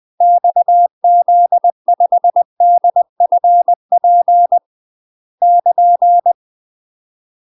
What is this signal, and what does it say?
Morse code, heard as a keyed tone, about 20 wpm, 700 Hz. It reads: XZ5DFP <KN>
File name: UK MASTER QSOs TRAINING / 20WPM / UK MASTER QSOs 0025